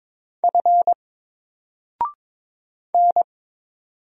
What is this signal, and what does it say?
Morse code, heard as a keyed tone, about 22 wpm, 700 Hz. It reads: F N
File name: Single-Letter-Number-22wpm